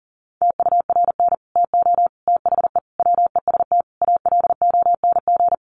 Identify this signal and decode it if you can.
Morse code, heard as a keyed tone, about 40 wpm, 700 Hz. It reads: TURN TO THE WEST ALONG
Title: Sentences from Top 1200 Words 40wpm